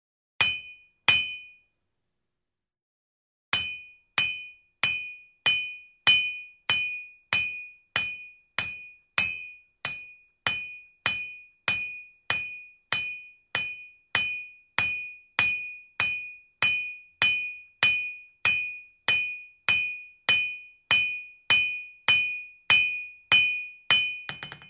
There is no code.